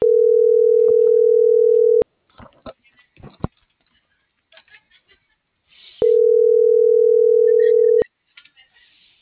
Ringback tone (North America), 440 + 480 Hz.